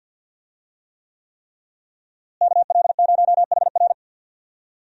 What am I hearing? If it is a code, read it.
Morse code, heard as a keyed tone, about 51 wpm, 700 Hz. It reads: KL0HR
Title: USA Call Signs - 5 Characters - Repeated 3x - 50wpm